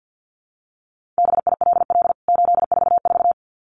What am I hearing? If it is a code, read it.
Morse code, heard as a keyed tone, about 49 wpm, 700 Hz. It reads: BILL 844
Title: CWT Exchange 50wpm